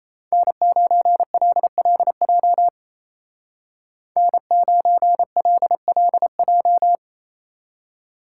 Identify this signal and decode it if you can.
Morse code, heard as a keyed tone, about 34 wpm, 700 Hz. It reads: N9LLJ N9LLJ
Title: Speed Racing - USA 1x3 Call Signs 28wpm